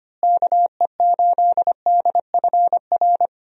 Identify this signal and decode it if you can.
Morse code, heard as a keyed tone, about 25 wpm, 700 Hz. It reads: KE8DFR
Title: USA Call Signs - 6 Characters - Repeated 3x - 25wpm